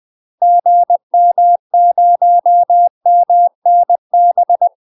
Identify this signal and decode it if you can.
Morse code, heard as a keyed tone, about 20 wpm, 700 Hz. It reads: GM0MNB
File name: UK MASTER QSOs TRAINING / 20WPM / UK MASTER QSOs 0042